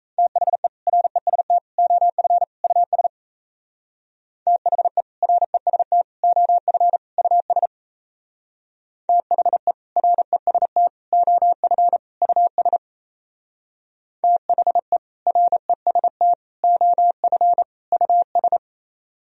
Morse code, heard as a keyed tone, about 34 wpm, 700 Hz. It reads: THE REST OF US THE REST OF US THE REST OF US THE REST OF US
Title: Speed-Racing - Sets of 4 Words 28wpm